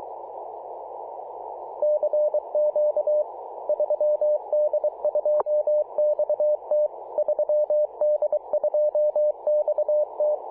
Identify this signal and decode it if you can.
Morse code, heard as a keyed tone, about 23 wpm, 600 Hz. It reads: CQ 3D2XT 3D2XT
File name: posts / apr-13-2119z-3d2xt-21-021mhz-cw